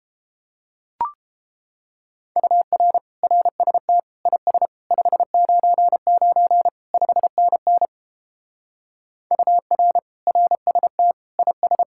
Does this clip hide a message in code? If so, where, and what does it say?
Yes: Morse code from 2.4 to 11.8 s, heard as a keyed tone, about 29 wpm, 700 Hz. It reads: UR RST IS 599 5NN UR RST IS